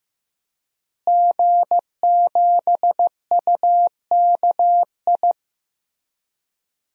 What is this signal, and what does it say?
Morse code, heard as a keyed tone, about 15 wpm, 700 Hz. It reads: G7UKI
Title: DXCC-100 Call Signs - 5 Characters - Repeated 3x - 15wpm